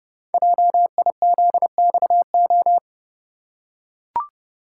Morse code, heard as a keyed tone, about 30 wpm, 700 Hz. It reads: JIZXO
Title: Code Groups - 5 Letters - 30wpm